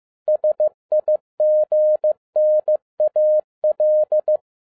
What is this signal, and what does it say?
Morse code, heard as a keyed tone, about 15 wpm, 610 Hz. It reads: SIGNAL
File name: 2011-07-mp3 / n42 EXCITING QRP CW OPERATION 15wpm